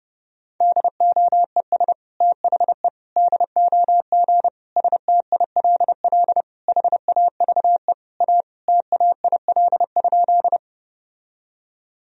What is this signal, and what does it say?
Morse code, heard as a keyed tone, about 32 wpm, 700 Hz. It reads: DOES THE DOG STILL HAVE A TAIL?
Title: Sentences from Top 1000 Words - Review Part 3 - 30wpm